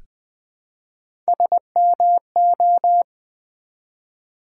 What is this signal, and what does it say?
Morse code, heard as a keyed tone, about 20 wpm, 700 Hz. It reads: SMO
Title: Build A Word - 1 Letter At A Time - 5-Letter Words - 20wpm